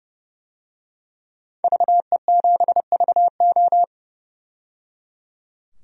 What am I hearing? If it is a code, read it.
Morse code, heard as a keyed tone, about 30 wpm, 700 Hz. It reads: VE7VO
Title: DXCC-100 Call Signs - 5 Characters - 30wpm